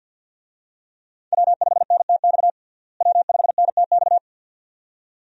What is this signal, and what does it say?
Morse code, heard as a keyed tone, about 46 wpm, 700 Hz. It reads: W5NTX W5NTX
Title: USA Call Signs - 5 Characters - Repeated 2x - 50wpm